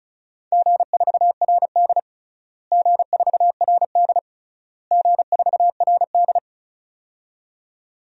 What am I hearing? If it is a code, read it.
Morse code, heard as a keyed tone, about 36 wpm, 700 Hz. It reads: G4RD G4RD G4RD
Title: DXCC-100 Call Signs - 4 Characters - Repeated 3x - 35wpm